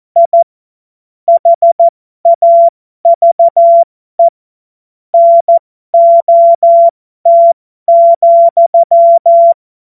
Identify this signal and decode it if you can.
Morse code, heard as a keyed tone, about 14 wpm, spaced as for 12 wpm, 680 Hz. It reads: I HAVE NOT,